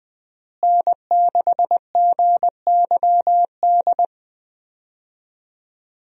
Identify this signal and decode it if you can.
Morse code, heard as a keyed tone, about 20 wpm, 700 Hz. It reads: N6GYD
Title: USA Call Signs - 5 Characters - Repeated 2x - 20wpm